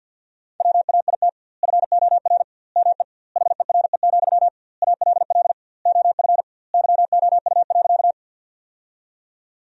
Morse code, heard as a keyed tone, about 47 wpm, 700 Hz. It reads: WAIT FOR ME HERE, ALL OF YOU.
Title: 2x Word-Spacing - Sentences from Top 500 Words - 50wpm